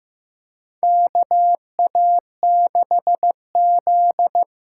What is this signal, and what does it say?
Morse code, heard as a keyed tone, about 15 wpm, 700 Hz. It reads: KA6Z
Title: USA Call Signs - 6 Characters - 8x Character Spacing - 15wpm